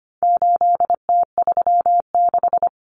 Morse code, heard as a keyed tone, about 25 wpm, 700 Hz. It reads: 8T36
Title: Common Cut Numbers - 4-Digit 25wpm